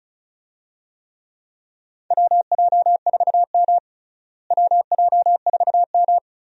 Morse code, heard as a keyed tone, about 36 wpm, 700 Hz. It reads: WJ4M WJ4M
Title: USA Call Signs - 4 Characters - Repeated 3x - 35wpm